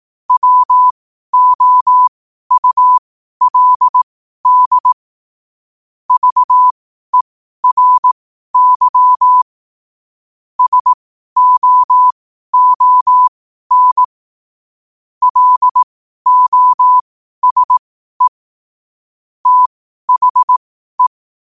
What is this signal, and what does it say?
Morse code, heard as a keyed tone, about 18 wpm, spaced as for 12 wpm, 1000 Hz. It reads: WOULD VERY SOON LOSE THE
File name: qotdfiles / 12wpm